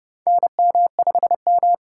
Morse code, heard as a keyed tone, about 30 wpm, 700 Hz. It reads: NM5M
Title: USA Call Signs - 4 Characters - 2x Character Spacing - 30wpm